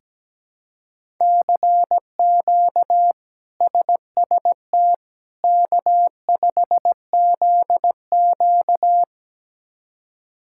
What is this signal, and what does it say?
Morse code, heard as a keyed tone, about 17 wpm, 700 Hz. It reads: CQ SST K5ZQ
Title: K1USN SST QSO Practice - 17wpm